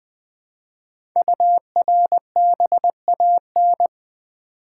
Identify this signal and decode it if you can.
Morse code, heard as a keyed tone, about 20 wpm, 700 Hz. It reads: URBAN
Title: Build A Word - 1 Letter At A Time - 5-Letter Words - 20wpm